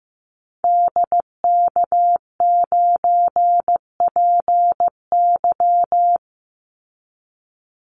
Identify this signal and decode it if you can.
Morse code, heard as a keyed tone, about 15 wpm, 700 Hz. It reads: DK9PY
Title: Call Signs Repeated 2x - 15wpm